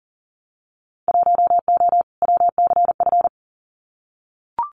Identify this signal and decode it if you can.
Morse code, heard as a keyed tone, about 40 wpm, 700 Hz. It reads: 1O WCF
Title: Winter Field Day Exchanges 40wpm